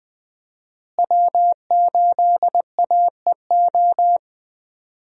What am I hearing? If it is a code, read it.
Morse code, heard as a keyed tone, about 20 wpm, 700 Hz. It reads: W8AEO
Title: USA Call Signs - 5 Characters - Repeated 2x - 20wpm